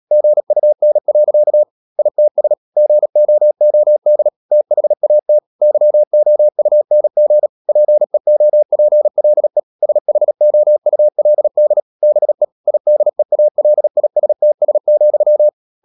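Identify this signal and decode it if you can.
Morse code, heard as a keyed tone, about 37 wpm, 600 Hz. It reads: GUN. ITS GOOD THAT YOUNG PEOPLE SHOULD BE IDEALISTS,
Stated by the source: Morse code practice at 35–39 wpm sending English text of Ken Follett's novel Never, from CW med Gnister.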